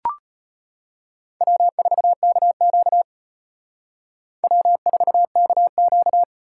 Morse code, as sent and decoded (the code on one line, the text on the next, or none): .-- ....- -.- --.- / .-- ....- -.- --.-
W4KQ W4KQ